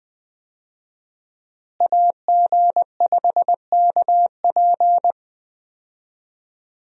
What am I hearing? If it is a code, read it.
Morse code, heard as a keyed tone, about 21 wpm, 700 Hz. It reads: AG5KP
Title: USA Call Signs - 5 Characters - Repeated 2x - 20wpm